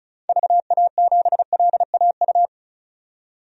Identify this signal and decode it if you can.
Morse code, heard as a keyed tone, about 36 wpm, 700 Hz. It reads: VA7LAU
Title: DXCC-100 Call Signs - 6 Characters - 35wpm